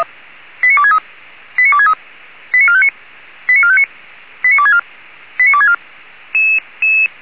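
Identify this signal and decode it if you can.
Five-tone selective call (ZVEI), heard as quick runs of five tones, tones of about 70 ms. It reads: 89263 89263 89459 89459 89264 89264
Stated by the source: ZVEI five-tone call 89273 received by SDR.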